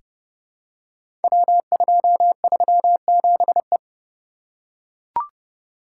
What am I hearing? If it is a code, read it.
Morse code, heard as a keyed tone, about 29 wpm, 700 Hz. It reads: W237E
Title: Code Groups - 5 Letter:Numbers - 30wpm